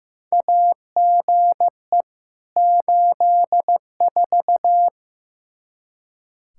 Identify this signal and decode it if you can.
Morse code, heard as a keyed tone, about 15 wpm, 700 Hz. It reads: AGE 84
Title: Practice QSO - Exchange 3 - More Info - 15wpm